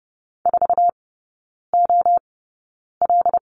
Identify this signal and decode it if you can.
Morse code, heard as a keyed tone, about 30 wpm, spaced as for 9 wpm, 700 Hz. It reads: 4OL
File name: Lesson 16e - Letter c - Three Characters 30wpm